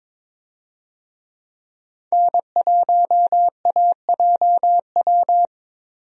Morse code, heard as a keyed tone, about 22 wpm, 700 Hz. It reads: N1AJW
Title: USA Call Signs - 5 Characters - Repeated 2x - 22wpm